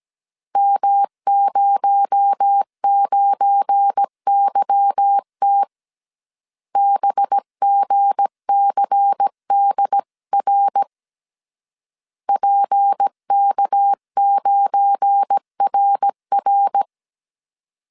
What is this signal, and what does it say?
Morse code, heard as a keyed tone, about 17 wpm, 800 Hz. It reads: M09YT BGCDR PK9RR
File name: CW / DET / 1 9 1500C S A 16wpm 2828